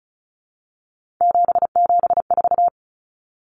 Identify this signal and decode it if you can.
Morse code, heard as a keyed tone, about 34 wpm, 700 Hz. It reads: 774
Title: Common Cut Numbers - 3-Digit 35wpm